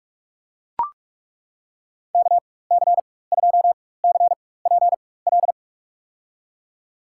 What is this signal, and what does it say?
Morse code, heard as a keyed tone, about 43 wpm, spaced as for 21 wpm, 700 Hz. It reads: KC2CPL
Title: USA Call Signs - 4x Character Spacing - 45wpm